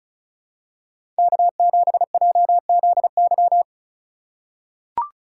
Morse code, heard as a keyed tone, about 36 wpm, 700 Hz. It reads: K7JZY
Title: USA Call Signs - 5 Characters - 3x Character Spacing - 35wpm